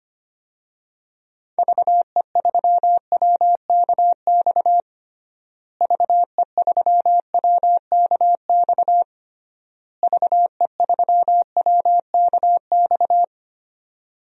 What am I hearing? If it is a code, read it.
Morse code, heard as a keyed tone, about 25 wpm, 700 Hz. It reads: VE3WKX VE3WKX VE3WKX
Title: DXCC-100 Call Signs - 6 Characters - Repeated 3x - 25wpm